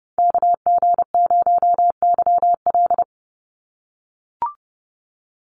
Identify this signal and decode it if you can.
Morse code, heard as a keyed tone, about 30 wpm, 700 Hz. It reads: KG0YL
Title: Call Signs - 30wpm